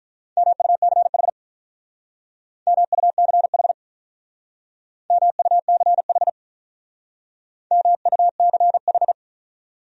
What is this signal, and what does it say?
Morse code, heard as a keyed tone, about 46 wpm, 700 Hz. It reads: MUCH MUCH MUCH MUCH